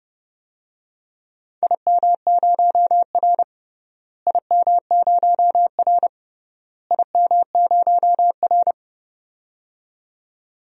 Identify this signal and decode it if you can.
Morse code, heard as a keyed tone, about 30 wpm, 700 Hz. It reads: IM0R IM0R IM0R
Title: DXCC-100 Call Signs - 4 Characters - Repeated 3x - 30wpm